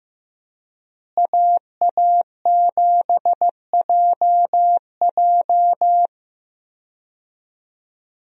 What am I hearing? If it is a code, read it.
Morse code, heard as a keyed tone, about 15 wpm, 700 Hz. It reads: AA7JJ
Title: USA Call Signs - 5 Characters - Repeated 3x - 15wpm